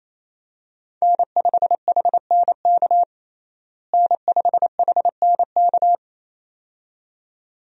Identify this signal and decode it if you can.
Morse code, heard as a keyed tone, about 27 wpm, 700 Hz. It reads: N5HNK N5HNK
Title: USA Call Signs - 5 Characters - Repeated 2x - 28wpm